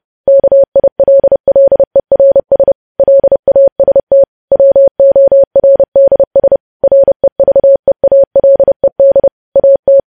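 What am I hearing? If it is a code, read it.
Morse code, heard as a keyed tone, about 30 wpm, 550 Hz. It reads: KILLERS LAST WORDS REVEALED AT
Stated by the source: News headlines sent in Morse code at 30.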